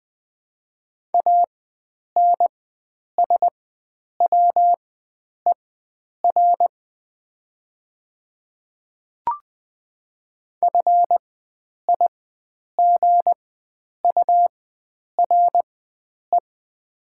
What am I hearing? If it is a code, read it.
Morse code, heard as a keyed tone, about 20 wpm, spaced as for 9 wpm, 700 Hz. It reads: ANSWER FIGURE
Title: 4x Character Spacing - Top 500 Words - 20wpm